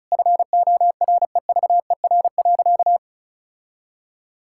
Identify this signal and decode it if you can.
Morse code, heard as a keyed tone, about 36 wpm, 700 Hz. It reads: FOREVER.
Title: Sentences from Top 1600 Words 35wpm